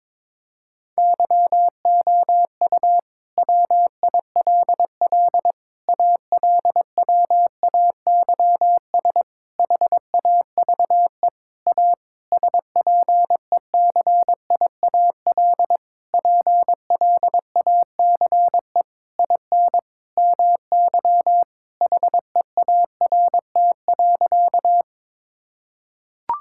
Morse code, heard as a keyed tone, about 22 wpm, 700 Hz. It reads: YOU WILL ALWAYS HAVE A SPECIAL PLACE IN MY HEART.
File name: Sentences with 9-Word Phrases - 22wpm